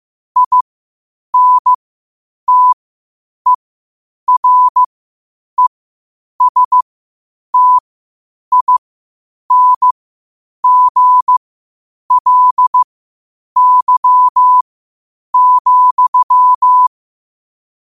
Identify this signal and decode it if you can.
Morse code, heard as a keyed tone, about 15 wpm, spaced as for 8 wpm, 1000 Hz. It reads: INTERESTINGLY,
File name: qotdfiles / 8wpm